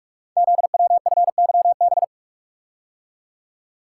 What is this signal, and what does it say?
Morse code, heard as a keyed tone, about 44 wpm, 700 Hz. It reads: ZWFYB
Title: Code Groups - 5 Letters - 45wpm